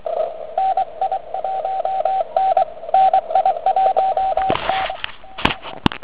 Morse code, heard as a keyed tone, about 23 wpm, 760 Hz, with radio noise. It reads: NI1N NI1T